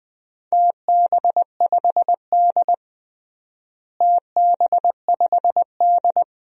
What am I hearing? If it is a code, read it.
Morse code, heard as a keyed tone, about 20 wpm, 700 Hz. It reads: TB5D TB5D
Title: DXCC-100 Call Signs - 4 Characters - Repeated 3x - 20wpm